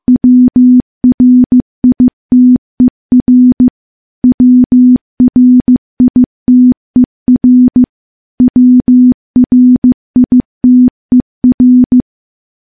Morse code, heard as a keyed tone, about 15 wpm, 250 Hz. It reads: WRITER WRITER WRITER
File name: morse / morse-training-15WPM